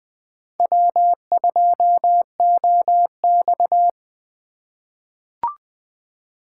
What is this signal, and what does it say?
Morse code, heard as a keyed tone, about 20 wpm, 700 Hz. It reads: W2OX
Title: USA Call Signs - 4 Characters - 2x Character Spacing - 20wpm